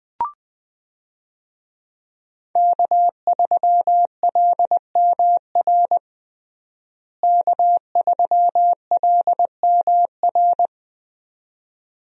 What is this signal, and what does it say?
Morse code, heard as a keyed tone, about 20 wpm, 700 Hz. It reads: K3LMR K3LMR
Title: USA Call Signs - 5 Characters - Repeated 2x - 20wpm